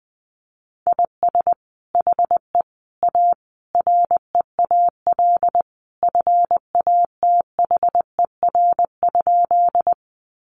Morse code, heard as a keyed tone, about 20 wpm, 700 Hz. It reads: IS HE A REAL FATHER?
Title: Sentences from Top 300 Words 20wpm